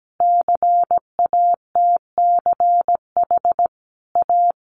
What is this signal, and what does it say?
Morse code, heard as a keyed tone, about 17 wpm, 700 Hz. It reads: CATCH A
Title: Sentences from Top 1500 Words 17wpm